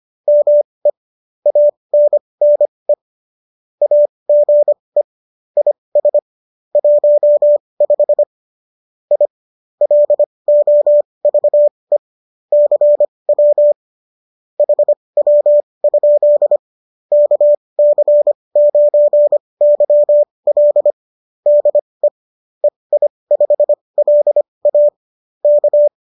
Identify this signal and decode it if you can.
Morse code, heard as a keyed tone, about 25 wpm, spaced as for 20 wpm, 590 Hz. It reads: ME ANNE AGE IS 15 I LOVE CW HW? KC9YL DE EI5LA K